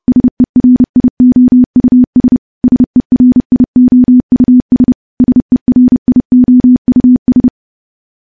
Morse code, heard as a keyed tone, about 30 wpm, 250 Hz. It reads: SERIOUS SERIOUS SERIOUS
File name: morse / morse-training-30WPM